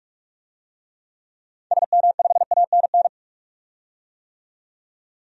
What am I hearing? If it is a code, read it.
Morse code, heard as a keyed tone, about 42 wpm, 700 Hz. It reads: SM5ANN
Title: DXCC-100 Call Signs - 6 Characters - 45wpm